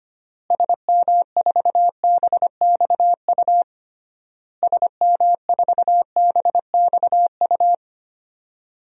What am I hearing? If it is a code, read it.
Morse code, heard as a keyed tone, about 25 wpm, 700 Hz. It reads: SM4BXU SM4BXU
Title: DXCC-100 Call Signs - 6 Characters - Repeated 3x - 25wpm